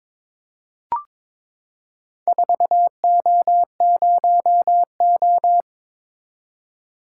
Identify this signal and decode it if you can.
Morse code, heard as a keyed tone, about 21 wpm, 700 Hz. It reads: 4O0O
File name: practice / 22wpm / Four Letter-Number Combinations - 22wpm